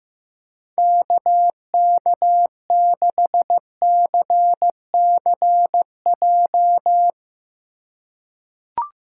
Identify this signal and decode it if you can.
Morse code, heard as a keyed tone, about 15 wpm, 700 Hz. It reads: KK6CCJ
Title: USA Call Signs - 4x Character Spacing - 15wpm